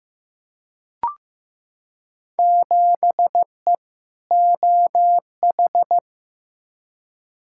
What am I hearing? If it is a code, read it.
Morse code, heard as a keyed tone, about 15 wpm, 700 Hz. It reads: 7E OH
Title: Field Day Exchanges Encore - 15wpm